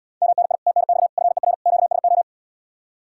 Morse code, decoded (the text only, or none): MNI TNX FER QSO